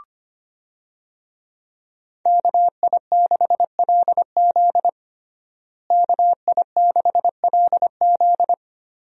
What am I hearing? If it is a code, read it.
Morse code, heard as a keyed tone, about 25 wpm, 700 Hz. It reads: KI6LZ KI6LZ
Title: USA Call Signs - 5 Characters - Repeated 3x - 25wpm